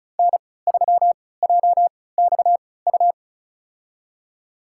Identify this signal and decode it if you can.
Morse code, heard as a keyed tone, about 34 wpm, spaced as for 20 wpm, 700 Hz. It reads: N3JXU
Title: USA Call Signs - 3x Character Spacing - 35wpm